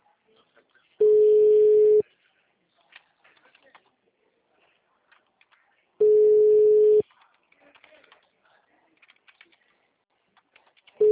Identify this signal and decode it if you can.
Ringback tone (Europe, 425 Hz).